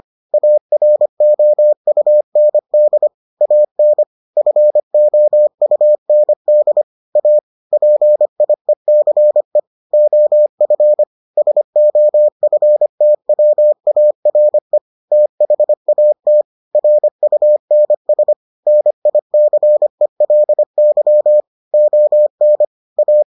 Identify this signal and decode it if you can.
Morse code, heard as a keyed tone, about 25 wpm, 600 Hz. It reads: AROUND AN FOUND A PIECE OF SOFTWARE THAT RUNS NICELY ON A